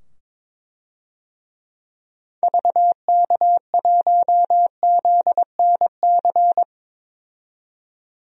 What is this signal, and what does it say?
Morse code, heard as a keyed tone, about 22 wpm, 700 Hz. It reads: VK1ZNC
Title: DXCC-100 Call Signs - 6 Characters - 22wpm